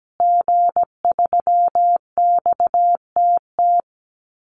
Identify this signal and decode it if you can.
Morse code, heard as a keyed tone, about 17 wpm, 700 Hz. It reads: G3XTT
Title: Call Signs Repeated 2x - 17wpm